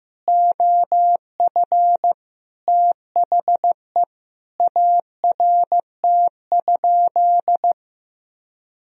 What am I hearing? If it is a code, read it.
Morse code, heard as a keyed tone, about 16 wpm, 700 Hz. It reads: OF THE ART?
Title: Sentences from Top 1000 Words - Review Part 3 - 15wpm